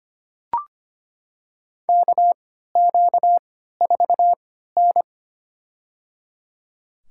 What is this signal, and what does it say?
Morse code, heard as a keyed tone, about 25 wpm, spaced as for 14 wpm, 700 Hz. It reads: KQ4N
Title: USA Call Signs - 3x Character Spacing - 25wpm